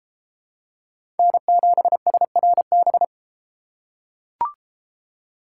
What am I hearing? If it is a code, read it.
Morse code, heard as a keyed tone, about 32 wpm, 700 Hz. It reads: N7SRB